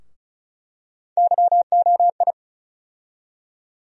Morse code, heard as a keyed tone, about 36 wpm, 700 Hz. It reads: YOI
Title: USA Call Sign Postfixes - 3 Letters 35wpm